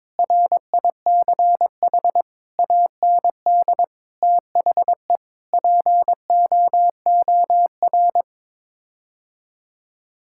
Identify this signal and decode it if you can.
Morse code, heard as a keyed tone, about 22 wpm, 700 Hz. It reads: RICH AND THE POOR
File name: practice / 22wpm / Sets of 8 Words - 22wpm